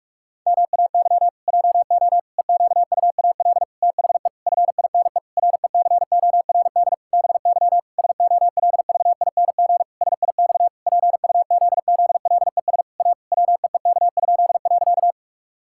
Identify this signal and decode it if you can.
Morse code, heard as a keyed tone, about 46 wpm, 700 Hz. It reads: MAY JO EQUAL THE FINE RECORD BY SOLVING SIX PUZZLES A WEEK?.
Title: Pangram Challenge v2 - 45wpm